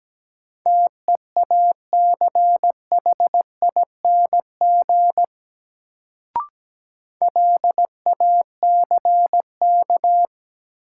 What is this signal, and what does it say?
Morse code, heard as a keyed tone, about 17 wpm, 700 Hz. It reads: TEACHING LACK
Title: Sentences from Top 1100 Words - New Word Drill Alt - 17wpm